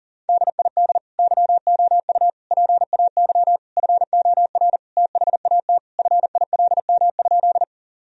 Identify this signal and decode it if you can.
Morse code, heard as a keyed tone, about 40 wpm, 700 Hz. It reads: DID YOU PAY FOR THAT FILM?
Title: Sentences from Top 1000 Words - Review Part 3 - 40wpm